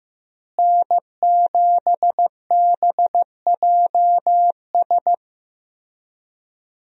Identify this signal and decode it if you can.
Morse code, heard as a keyed tone, about 16 wpm, 700 Hz. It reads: N7BJS
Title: USA Call Signs - 5 Characters - Repeated 2x - 15wpm